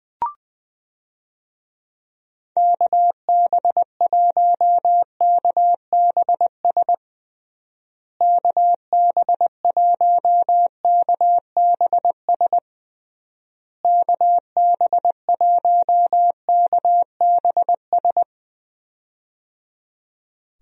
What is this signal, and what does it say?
Morse code, heard as a keyed tone, about 21 wpm, 700 Hz. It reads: KB1KBS KB1KBS KB1KBS
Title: USA Call Signs - 6 Characters - Repeated 3x - 20wpm